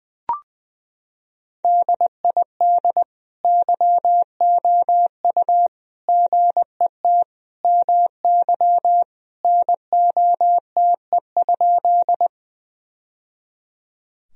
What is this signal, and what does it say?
Morse code, heard as a keyed tone, about 20 wpm, 700 Hz. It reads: DID YOU GET MY NOTE?
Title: Sentences from Top 1000 Words - Review Part 2 - 20wpm